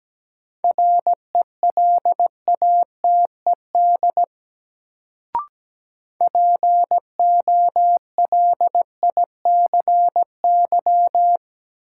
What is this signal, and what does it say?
Morse code, heard as a keyed tone, about 17 wpm, 700 Hz. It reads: RELATED POLICY
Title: Sentences from Top 1000 Words - Review Part 5 - New Word Drill - 17wpm